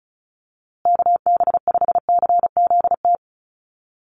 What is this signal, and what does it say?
Morse code, heard as a keyed tone, about 34 wpm, 700 Hz. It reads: KB5CZT
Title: USA Call Signs Repeated 2x 35wpm